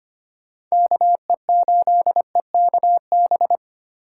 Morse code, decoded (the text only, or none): KE8EKB